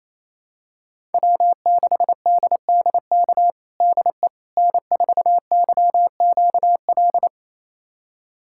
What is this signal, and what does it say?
Morse code, heard as a keyed tone, about 28 wpm, 700 Hz. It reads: W6DDK DE N4YQL
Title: USA Callsign DE Callsign - 5 Characters - 28wpm